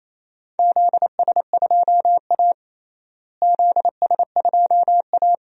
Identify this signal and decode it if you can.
Morse code, heard as a keyed tone, about 28 wpm, 700 Hz. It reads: ZS2A ZS2A
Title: DXCC-100 Call Signs - 4 Characters - Repeated 2x - 28wpm